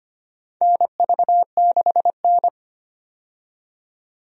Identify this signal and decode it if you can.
Morse code, heard as a keyed tone, about 25 wpm, 700 Hz. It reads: NV6N
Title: USA Call Signs - 4 Characters - 25wpm